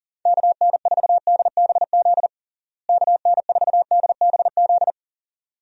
Morse code, heard as a keyed tone, about 40 wpm, 700 Hz. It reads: KN4DBZ KN4DBZ
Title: USA Call Signs - 6 Characters - Repeated 2x - 40wpm